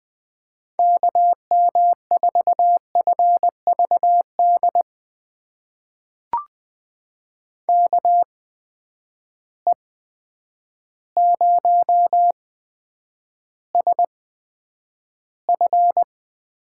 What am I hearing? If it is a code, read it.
Morse code, heard as a keyed tone, about 20 wpm, spaced as for 9 wpm, 700 Hz. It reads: KM4FVD KE0SF
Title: USA Call Signs - 6 Characters - 8x Character Spacing - 20wpm